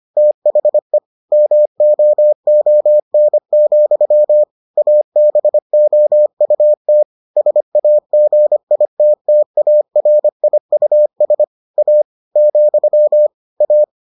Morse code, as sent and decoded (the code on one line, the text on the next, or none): - .... / -- --- --- -. --..-- / .- -... --- ..- - / ... .- --. .. - - .- .-. .. ..- ... / .- / --..-- / .-
THE MOON, ABOUT SAGITTARIUS A , A